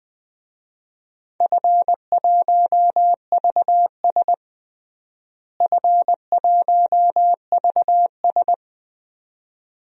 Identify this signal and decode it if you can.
Morse code, heard as a keyed tone, about 20 wpm, 700 Hz. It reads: F1VS F1VS
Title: DXCC-100 Call Signs - 4 Characters - Repeated 2x - 20wpm